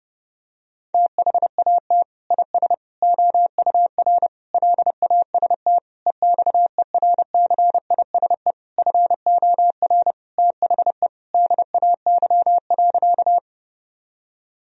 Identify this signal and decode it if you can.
Morse code, heard as a keyed tone, about 28 wpm, 700 Hz. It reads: THAT IS OUR LAST EXERCISE FOR THE DAY.
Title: Sentences from Top 1000 Words - Review Part 4 - 30wpm